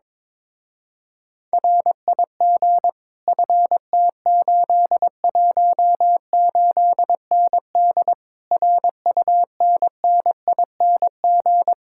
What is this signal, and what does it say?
Morse code, heard as a keyed tone, about 22 wpm, 700 Hz. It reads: RIG FT818ND RUNNING
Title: Speed-Racing Common QSO Phrases - 22wpm